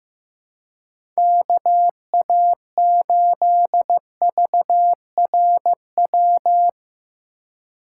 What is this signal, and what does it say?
Morse code, heard as a keyed tone, about 15 wpm, 700 Hz. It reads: KA8VRW
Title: USA Call Signs - 2x Character Spacing - 15wpm